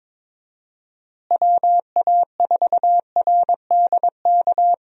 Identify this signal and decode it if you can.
Morse code, heard as a keyed tone, about 23 wpm, 700 Hz. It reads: WA4RDK
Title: USA Call Signs - 6 Characters - Repeated 3x - 22wpm